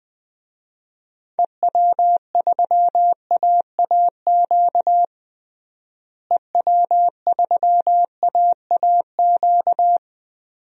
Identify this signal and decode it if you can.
Morse code, heard as a keyed tone, about 20 wpm, 700 Hz. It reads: EW3AAQ EW3AAQ
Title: DXCC-100 Call Signs - 6 Characters - Repeated 2x - 20wpm